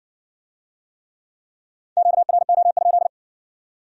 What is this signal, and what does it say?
Morse code, heard as a keyed tone, about 53 wpm, 700 Hz. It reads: QRO?